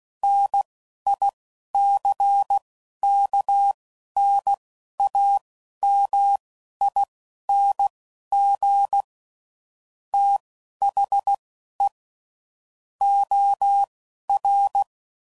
Morse code, heard as a keyed tone, about 16 wpm, spaced as for 11 wpm, 800 Hz. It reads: NICKNAMING THE OR
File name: CW Code Groups @10WPM / Technical Subjects / OR Gate